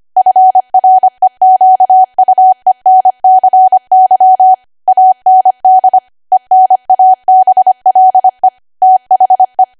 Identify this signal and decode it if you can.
Morse code, heard as a keyed tone, about 25 wpm, 750 Hz. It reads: FREQUENCY AND ENABLE THE
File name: Archive / 25 WPM / 25WPM